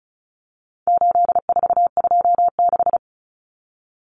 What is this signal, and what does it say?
Morse code, heard as a keyed tone, about 34 wpm, 700 Hz. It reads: 8426
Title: Numbers - 4-Digit 35wpm